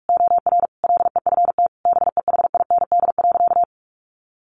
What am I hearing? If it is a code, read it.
Morse code, heard as a keyed tone, about 44 wpm, 700 Hz. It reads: OR LEFT BEHIND.
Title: Sentences from Top 1400 Words 45wpm